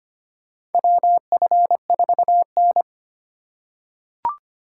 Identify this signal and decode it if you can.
Morse code, heard as a keyed tone, about 25 wpm, 700 Hz. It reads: WF4N
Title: USA Call Signs - 4 Characters - 2x Character Spacing - 25wpm